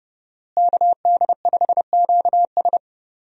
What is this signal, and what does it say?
Morse code, heard as a keyed tone, about 30 wpm, 700 Hz. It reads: KD5QS
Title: USA Call Signs - 5 Characters - Repeated 2x - 30wpm